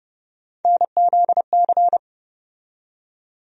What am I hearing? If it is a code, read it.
Morse code, heard as a keyed tone, about 30 wpm, 700 Hz. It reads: NZC
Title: USA Call Sign Postfixes - 3 Letters 30wpm